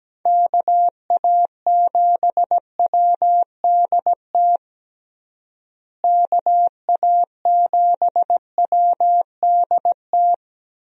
Morse code, heard as a keyed tone, about 17 wpm, 700 Hz. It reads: KA7WDT KA7WDT
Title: USA Call Signs - 6 Characters - Repeated 3x - 17wpm